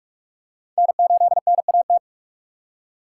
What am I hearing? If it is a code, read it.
Morse code, heard as a keyed tone, about 44 wpm, 700 Hz. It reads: N8NAT
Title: USA Call Signs - 5 Characters - 3x Character Spacing - 45wpm